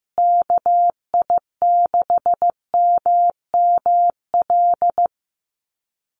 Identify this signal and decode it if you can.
Morse code, heard as a keyed tone, about 15 wpm, 700 Hz. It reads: KI6MML
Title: USA Call Signs Repeated 3x 15wpm